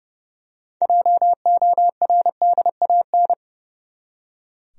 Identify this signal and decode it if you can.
Morse code, heard as a keyed tone, about 30 wpm, 700 Hz. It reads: JORDAN
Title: USA Common Names - 30wpm